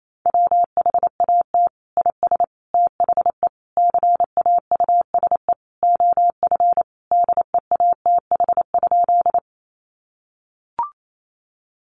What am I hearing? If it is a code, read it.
Morse code, heard as a keyed tone, about 28 wpm, 700 Hz. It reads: WHAT IS THE CAUSE OF DEATH?
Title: Sentences from Top 1100 Words 28wpm